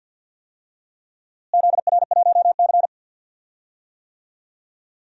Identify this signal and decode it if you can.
Morse code, heard as a keyed tone, about 47 wpm, 700 Hz. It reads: ZR1C
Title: DXCC-100 Call Signs - 4 Characters - Repeated 3x - 50wpm